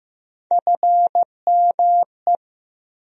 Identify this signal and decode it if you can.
Morse code, heard as a keyed tone, about 16 wpm, 700 Hz. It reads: FME
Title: USA Call Sign Postfixes - 3 Letters 15wpm